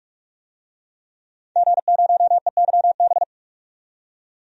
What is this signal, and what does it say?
Morse code, heard as a keyed tone, about 44 wpm, 700 Hz. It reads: G0EYB
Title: DXCC-100 Call Signs - 5 Characters - 45wpm